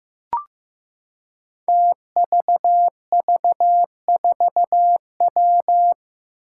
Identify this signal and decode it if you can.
Morse code, heard as a keyed tone, about 15 wpm, 700 Hz. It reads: TVV4W
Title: Five Challenging Letter-Number Combinations 15wpm